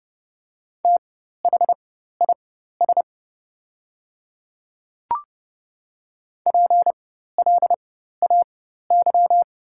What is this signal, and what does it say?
Morse code, heard as a keyed tone, about 30 wpm, spaced as for 14 wpm, 690 Hz. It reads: THIS PLAY
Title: 4x Character Spacing - Top 500 Words - 30wpm